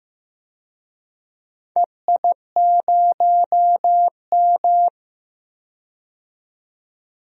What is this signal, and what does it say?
Morse code, heard as a keyed tone, about 15 wpm, 700 Hz. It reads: EI0M
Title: DXCC-100 Call Signs - 4 Characters - 15wpm